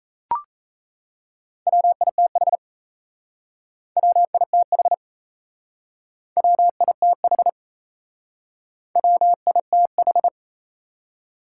Morse code, heard as a keyed tone, about 36 wpm, 700 Hz. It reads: WITH WITH WITH WITH